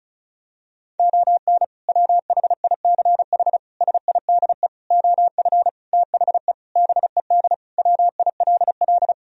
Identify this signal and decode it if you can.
Morse code, heard as a keyed tone, about 36 wpm, 700 Hz. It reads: ON WHICH SIDE OF THE BED WILL
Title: Sentences from Top 1000 Words - Review Part 1 - 35wpm